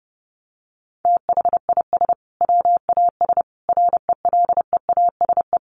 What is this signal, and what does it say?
Morse code, heard as a keyed tone, about 30 wpm, 700 Hz. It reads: THIS WAS RELEASE
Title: Sentences from Top 1100 Words 30wpm